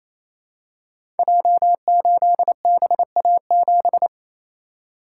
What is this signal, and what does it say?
Morse code, heard as a keyed tone, about 27 wpm, 700 Hz. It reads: J8BA7
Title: Code Groups - 5 Letter:Numbers - 28wpm